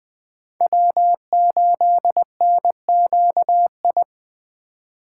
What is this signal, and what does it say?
Morse code, heard as a keyed tone, about 20 wpm, 700 Hz. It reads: W8NQI
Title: Speed Racing - USA 1x3 Call Signs 17wpm